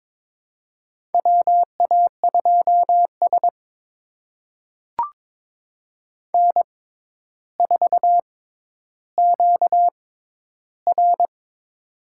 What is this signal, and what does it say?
Morse code, heard as a keyed tone, about 22 wpm, spaced as for 11 wpm, 700 Hz. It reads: WA2S N4QR
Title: USA Call Signs - 4 Characters - 6x Character Spacing - 22wpm